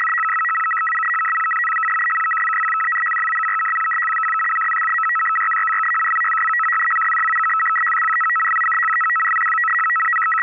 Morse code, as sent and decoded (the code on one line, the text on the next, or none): none